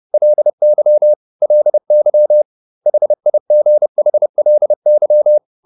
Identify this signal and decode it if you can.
Morse code, heard as a keyed tone, about 30 wpm, 600 Hz. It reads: LY LY HIGHLY